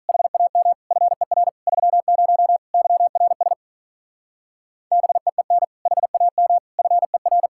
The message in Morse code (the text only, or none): HAM FER 30 YRS BEEN HAM FER